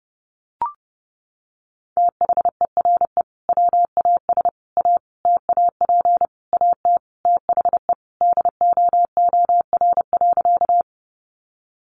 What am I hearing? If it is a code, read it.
Morse code, heard as a keyed tone, about 30 wpm, 700 Hz. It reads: THERE WAS A TAP AT THE DOOR.
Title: Sentences from Top 1400 Words 30wpm